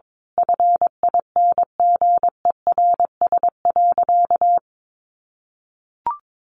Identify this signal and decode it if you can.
Morse code, heard as a keyed tone, about 22 wpm, 700 Hz. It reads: FINGERS.
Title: A Sound Of Thunder 22wpm